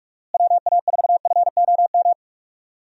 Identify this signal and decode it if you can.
Morse code, heard as a keyed tone, about 42 wpm, 700 Hz. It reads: WA4FOM
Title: USA Call Signs - 6 Characters - Repeated 3x - 45wpm